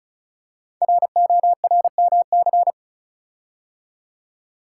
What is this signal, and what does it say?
Morse code, heard as a keyed tone, about 36 wpm, 700 Hz. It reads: RORMC
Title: Code Groups - 5 Letters - 35wpm